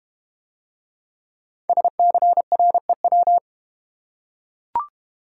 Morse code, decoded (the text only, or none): SCREW